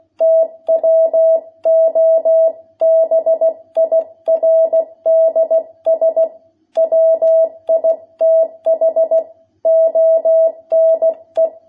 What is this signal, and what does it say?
Morse code, heard as a keyed tone, about 16 wpm, 660 Hz. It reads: TWO BIRDS WITH ONE
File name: jh8cbh / 2021cw